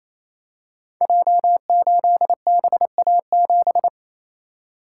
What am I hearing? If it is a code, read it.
Morse code, heard as a keyed tone, about 28 wpm, 700 Hz. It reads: J8BA7
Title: Code Groups - 5 Letter:Numbers - 28wpm